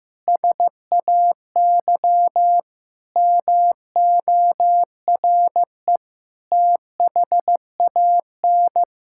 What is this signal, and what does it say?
Morse code, heard as a keyed tone, about 15 wpm, 700 Hz. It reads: SAY MORE THAN
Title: Sentences from Top 1600 Words 15wpm